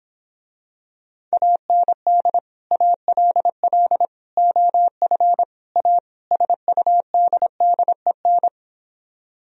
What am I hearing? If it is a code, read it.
Morse code, heard as a keyed tone, about 26 wpm, 700 Hz. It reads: AND ALL OF A SUDDEN